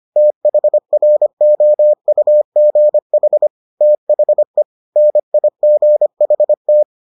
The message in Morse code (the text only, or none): THROUGH THE NIGHT